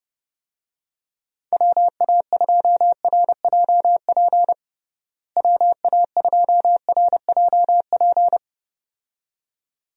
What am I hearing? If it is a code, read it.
Morse code, heard as a keyed tone, about 30 wpm, 700 Hz. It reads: WA2RJP WA2RJP
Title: USA Call Signs - 6 Characters - Repeated 2x - 30wpm